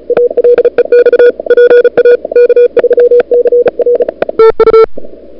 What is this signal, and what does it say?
Morse code, heard as a keyed tone, about 36 wpm, 440 Hz. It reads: ALEX PA K3CR TU
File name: n6tr / Fab2007CWSprintHour1